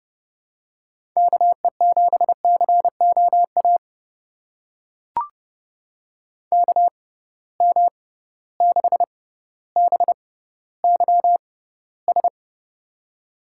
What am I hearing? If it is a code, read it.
Morse code, heard as a keyed tone, about 32 wpm, spaced as for 16 wpm, 700 Hz. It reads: KE7COA KM6BYS
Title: USA Call Signs - 6 Characters - 6x Character Spacing - 30wpm